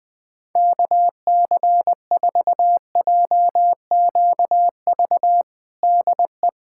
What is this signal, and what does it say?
Morse code, heard as a keyed tone, about 20 wpm, 700 Hz. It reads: KC4JQV DE
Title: USA Callsign DE Callsign - 6 Characters - 20wpm